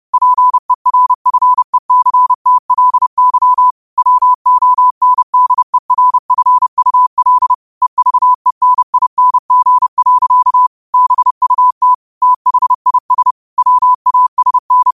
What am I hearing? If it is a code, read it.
Morse code, heard as a keyed tone, about 30 wpm, 1000 Hz. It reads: PERFECTLY WONDERFUL EVENING. BUT THIS WASN
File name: qotdfiles / 30wpm